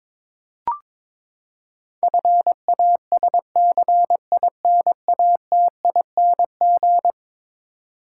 Morse code, heard as a keyed tone, about 21 wpm, 700 Hz. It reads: FASCINATING
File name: practice / 22wpm / Eleven Letter Words - 22wpm